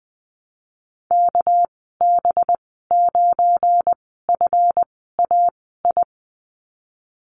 Morse code, heard as a keyed tone, about 20 wpm, spaced as for 15 wpm, 700 Hz. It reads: KB9FAI
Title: 219 Lesson 35g - Number 0 - Call Signs 20wpm